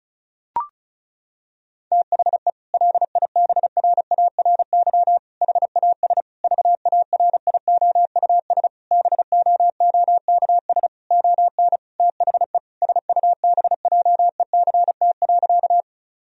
Morse code, heard as a keyed tone, about 36 wpm, 700 Hz. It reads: THE LIBRARY HAS VARIOUS BOOKS ON THE SUBJECT.
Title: Sentences from Top 1000 Words - Review Part 4 - 35wpm